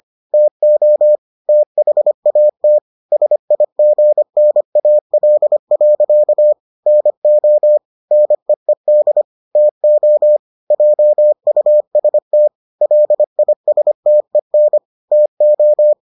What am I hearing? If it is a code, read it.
Morse code, heard as a keyed tone, about 25 wpm, 600 Hz. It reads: TO THAT SIGNAL. NO NEED TO JUST LISTEN TO